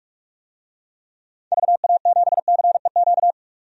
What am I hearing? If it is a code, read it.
Morse code, heard as a keyed tone, about 42 wpm, 700 Hz. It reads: VA7CEQ
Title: DXCC-100 Call Signs - 6 Characters - 45wpm